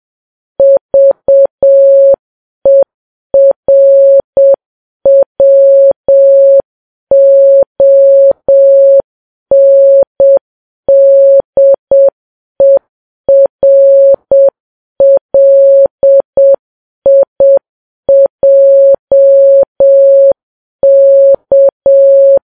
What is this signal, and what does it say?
Morse code, heard as a keyed tone, about 7 wpm, 550 Hz. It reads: VERWONDERLIJK